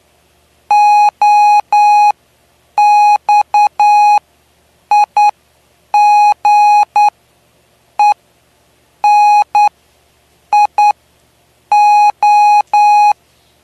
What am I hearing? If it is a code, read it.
Morse code, heard as a keyed tone, about 10 wpm, spaced as for 7 wpm, 830 Hz. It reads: OXIGENIO
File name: cw / py2lcd / 09-PY2LCD-Li��o 9